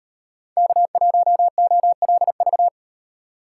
Morse code, heard as a keyed tone, about 38 wpm, 700 Hz. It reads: K1OLV